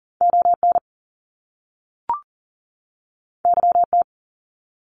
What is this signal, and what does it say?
Morse code, heard as a keyed tone, about 41 wpm, 700 Hz. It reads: ON YT
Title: Canadian Provinces and Territories 40wpm